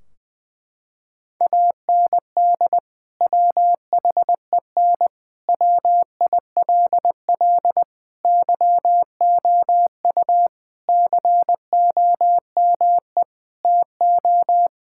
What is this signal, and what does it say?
Morse code, heard as a keyed tone, about 21 wpm, 700 Hz. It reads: AND WHEN WILL YOU COME TO
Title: Sentences from Top 1600 Words 20wpm